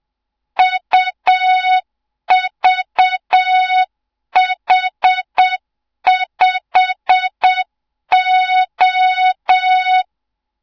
Morse code, heard as a keyed tone, about 7 wpm, 3700 Hz. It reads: UVH5O
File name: Morze-HA2MN / hang / 07 GYAKORLAT